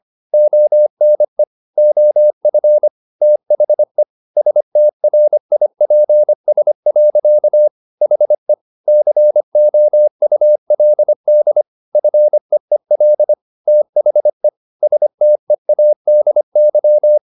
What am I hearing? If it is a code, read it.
Morse code, heard as a keyed tone, about 25 wpm, 600 Hz. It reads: ONE OF THE STRIPS. HE COULD FEEL THE STEADY
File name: the old man and the sea part 1 25wpm